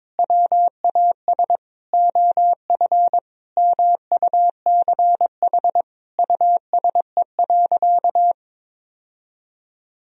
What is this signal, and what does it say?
Morse code, heard as a keyed tone, about 22 wpm, 700 Hz. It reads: WAS OF MUCH USE.
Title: Sentences from Top 1000 Words - Review Part 3 - 22wpm